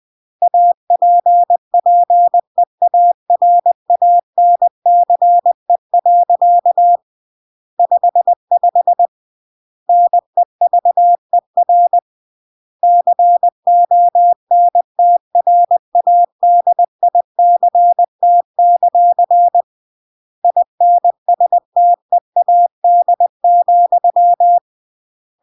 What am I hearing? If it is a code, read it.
Morse code, heard as a keyed tone, about 20 wpm, 700 Hz. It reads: APPEARANCE. 55 NEVER CONTRADICT; INSTEAD,
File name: CW / jodorowsky / 82 commandments 20-20